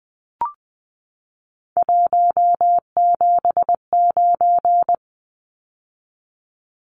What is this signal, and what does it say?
Morse code, heard as a keyed tone, about 20 wpm, 700 Hz. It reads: 179
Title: Numbers - 3-Digit 20wpm